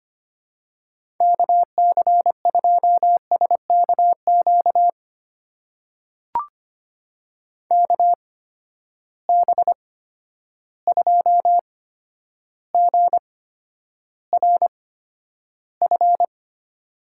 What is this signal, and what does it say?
Morse code, heard as a keyed tone, about 25 wpm, spaced as for 11 wpm, 700 Hz. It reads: KC2SKQ KB2GRF
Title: USA Call Signs - 6 Characters - 8x Character Spacing - 25wpm